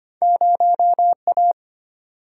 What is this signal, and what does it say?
Morse code, heard as a keyed tone, about 34 wpm, spaced as for 29 wpm, 700 Hz. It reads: 0A
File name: practice / 25wpm / Speed-Racing - Two Letter-Number Combinations - 25wpm